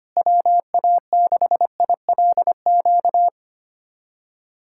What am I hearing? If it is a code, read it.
Morse code, heard as a keyed tone, about 25 wpm, 700 Hz. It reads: WA6ILQ
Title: USA Callsign DE Callsign - 6 Characters - 25wpm